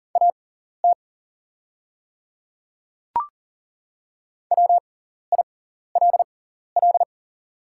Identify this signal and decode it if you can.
Morse code, heard as a keyed tone, about 40 wpm, spaced as for 14 wpm, 700 Hz. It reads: AT WILL
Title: 6x Character Spacing - Top 200 Words - 40wpm